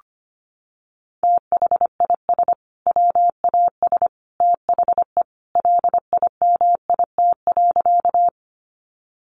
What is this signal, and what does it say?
Morse code, heard as a keyed tone, about 25 wpm, 700 Hz. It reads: THIS WAS THE LIMIT.
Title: Sentences from Top 1500 Words 25wpm